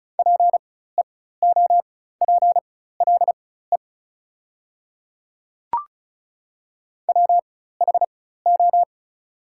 Morse code, heard as a keyed tone, about 36 wpm, spaced as for 16 wpm, 700 Hz. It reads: PEOPLE WHO
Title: 4x Character Spacing - Top 100 Words - 35wpm